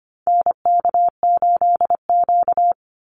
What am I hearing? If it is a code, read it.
Morse code, heard as a keyed tone, about 25 wpm, 700 Hz. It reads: NK8Q
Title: NA SOTA Activator Callsigns 25wpm